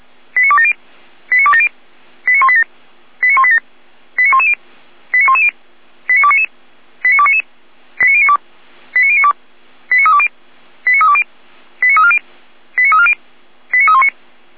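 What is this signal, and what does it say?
Five-tone selective call (ZVEI), heard as quick runs of five tones, tones of about 70 ms. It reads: 89270 89270 89187 89187 89110 89110 89299 89299 89092 89092 89320 89320 89340 89340 89319
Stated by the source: ZVEI five-tone selective call received with an SDR.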